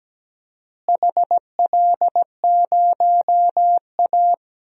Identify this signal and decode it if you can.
Morse code, heard as a keyed tone, about 17 wpm, 700 Hz. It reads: HL0A
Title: DXCC-100 Call Signs - 4 Characters - Repeated 2x - 17wpm